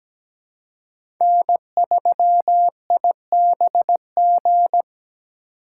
Morse code, heard as a keyed tone, about 17 wpm, 700 Hz. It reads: N3IBG
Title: USA Call Signs - 2x Character Spacing - 17wpm